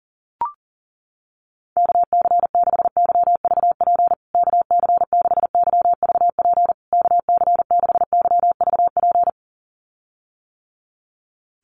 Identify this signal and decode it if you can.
Morse code, heard as a keyed tone, about 40 wpm, 700 Hz. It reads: KC6YVP KC6YVP KC6YVP
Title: USA Call Signs Repeated 3x 40wpm